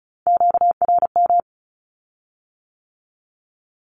Morse code, heard as a keyed tone, about 36 wpm, 700 Hz. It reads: QRM
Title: Q-Codes with Spoken Meaning - 35wpm